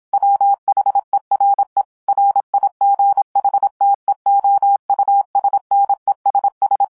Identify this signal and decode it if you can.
Morse code, heard as a keyed tone, about 26 wpm, 800 Hz. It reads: WHERE RIGHTEOUSNESS